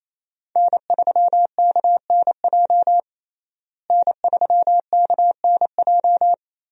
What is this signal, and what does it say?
Morse code, heard as a keyed tone, about 28 wpm, 700 Hz. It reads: N3KNJ N3KNJ
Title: USA Call Signs - 5 Characters - Repeated 2x - 28wpm